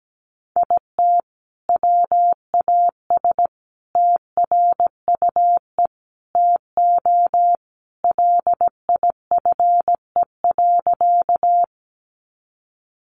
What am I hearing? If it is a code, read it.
Morse code, heard as a keyed tone, about 17 wpm, 700 Hz. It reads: IT WAS TRUE TO LIFE.
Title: Sentences from Top 400 Words 17wpm